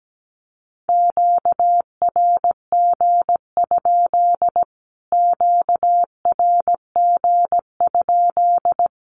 Morse code, heard as a keyed tone, about 17 wpm, 700 Hz. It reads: QRG? QRG?
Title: Q-Codes Repeated 3x 17wpm